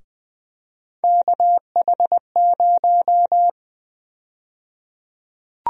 Morse code, heard as a keyed tone, about 20 wpm, 700 Hz. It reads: KH0
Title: USA Call Sign Prefixes - Repeated 2x - 20wpm